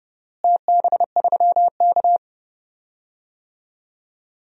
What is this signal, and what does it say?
Morse code, heard as a keyed tone, about 30 wpm, 700 Hz. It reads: TB3K
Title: DXCC-100 Call Signs - 4 Characters - Repeated 2x - 30wpm